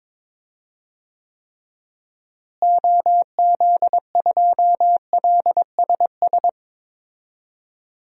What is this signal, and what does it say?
Morse code, heard as a keyed tone, about 21 wpm, 700 Hz. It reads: OZ2LSS
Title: DXCC-100 Call Signs - 6 Characters - Repeated 3x - 22wpm